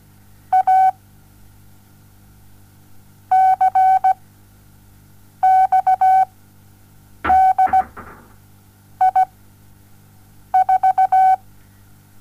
Morse code, heard as a keyed tone, about 17 wpm, 760 Hz. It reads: A C X D I 4
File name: Onlinelehrgang / morsen / DJ4UF Morsen07